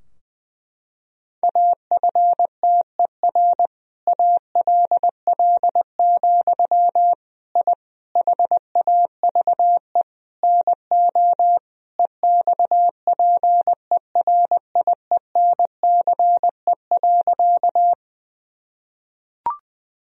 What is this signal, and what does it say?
Morse code, heard as a keyed tone, about 21 wpm, 700 Hz. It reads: AFTER ALL, I HAVE NO EXPERIENCE.
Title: Sentences from Top 1000 Words - Review Part 3 - 20wpm